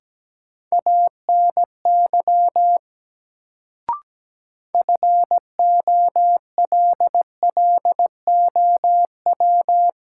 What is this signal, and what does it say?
Morse code, heard as a keyed tone, about 17 wpm, 700 Hz. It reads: ANY FOLLOW
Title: Sentences from Top 200 Words - New Word Drill - 17wpm